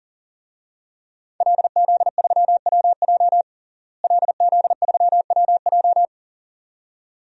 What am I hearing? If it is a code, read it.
Morse code, heard as a keyed tone, about 37 wpm, 700 Hz. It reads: LZ3WJ LZ3WJ
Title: DXCC-100 Call Signs - 5 Characters - Repeated 2x - 40wpm